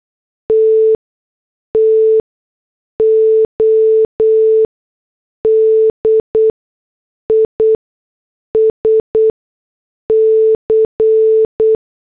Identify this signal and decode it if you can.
Morse code, heard as a keyed tone, about 8 wpm, spaced as for 6 wpm, 440 Hz. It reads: T TO DISC
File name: morse / headlines